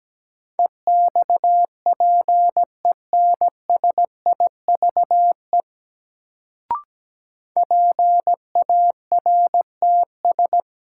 Morse code, heard as a keyed tone, about 17 wpm, 700 Hz. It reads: EXPENSIVE PARTS
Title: Sentences from Top 1000 Words - Review Part 6 - New Word Drill - 17wpm